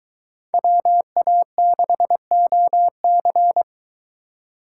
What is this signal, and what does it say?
Morse code, heard as a keyed tone, about 23 wpm, 700 Hz. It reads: WA6OC